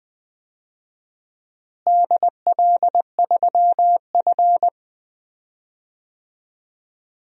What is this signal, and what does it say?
Morse code, heard as a keyed tone, about 20 wpm, 700 Hz. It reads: DL3F
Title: DXCC-100 Call Signs - 4 Characters - Repeated 2x - 20wpm